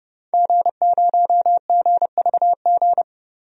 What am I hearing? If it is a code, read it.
Morse code, heard as a keyed tone, about 30 wpm, 700 Hz. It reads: G0GVG
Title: DXCC-100 Call Signs - 5 Characters - 30wpm